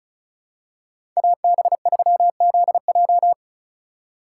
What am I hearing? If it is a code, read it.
Morse code, heard as a keyed tone, about 36 wpm, 700 Hz. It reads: AB3ZJ
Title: USA Call Signs - 5 Characters - 3x Character Spacing - 35wpm